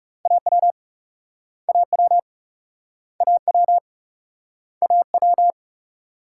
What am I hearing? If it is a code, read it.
Morse code, heard as a keyed tone, about 36 wpm, 700 Hz. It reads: AW AW AW AW
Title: Speed-Racing Multi-letter Phonemes - 30wpm